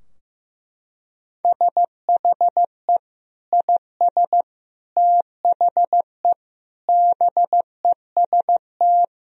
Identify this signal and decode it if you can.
Morse code, heard as a keyed tone, about 15 wpm, 700 Hz. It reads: SHE IS THE BEST
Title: Sentences from Top 1000 Words - Review Part 4 - 15wpm